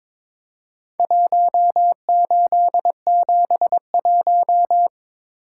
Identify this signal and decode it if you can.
Morse code, heard as a keyed tone, about 22 wpm, 700 Hz. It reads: 1871
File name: practice / 22wpm / Four Letter-Number Combinations - 22wpm